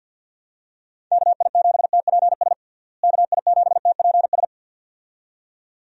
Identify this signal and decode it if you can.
Morse code, heard as a keyed tone, about 48 wpm, 700 Hz. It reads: KI7TPS KI7TPS
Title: USA Call Signs - 6 Characters - Repeated 2x - 50wpm